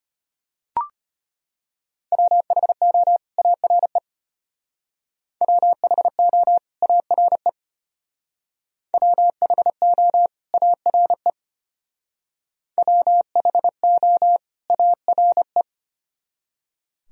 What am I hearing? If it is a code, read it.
Morse code, heard as a keyed tone, about 32 wpm, 700 Hz. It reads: WHO ARE WHO ARE WHO ARE WHO ARE